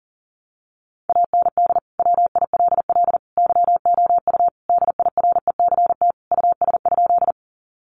Morse code, heard as a keyed tone, about 37 wpm, 700 Hz. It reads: AND WILL YOU DIRECT US?
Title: Sentences from Top 400 Words 40wpm